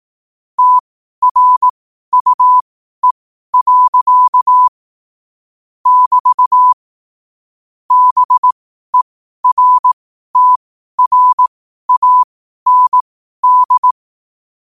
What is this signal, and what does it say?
Morse code, heard as a keyed tone, about 18 wpm, spaced as for 12 wpm, 1000 Hz. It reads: TRUE. <BT> BERTRAND
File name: qotdfiles / 12wpm